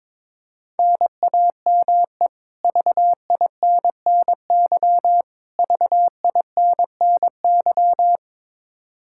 Morse code, heard as a keyed tone, about 22 wpm, 700 Hz. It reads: NAME VINNY VINNY
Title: K1USN SST QSO Practice 22wpm